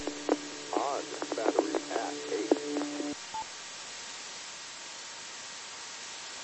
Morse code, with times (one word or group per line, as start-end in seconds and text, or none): none